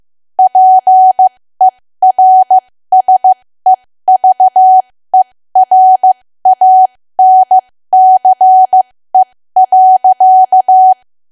Morse code, heard as a keyed tone, about 15 wpm, spaced as for 13 wpm, 750 Hz. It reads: PERSEVERANCE.
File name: cwa / W1AW-Level2 / 13WPM